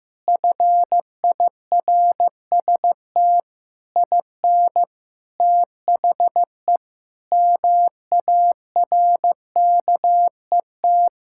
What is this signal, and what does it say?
Morse code, heard as a keyed tone, about 16 wpm, 700 Hz. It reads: FIRST IN THE MARKET
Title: Sentences from Top 1600 Words 15wpm